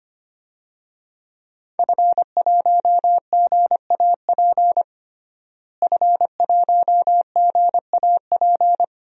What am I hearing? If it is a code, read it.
Morse code, heard as a keyed tone, about 25 wpm, 700 Hz. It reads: F1GAP F1GAP
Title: DXCC-100 Call Signs - 5 Characters - Repeated 3x - 25wpm